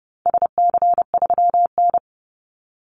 Morse code, heard as a keyed tone, about 30 wpm, 700 Hz. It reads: SC3N
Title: Call Signs - 30wpm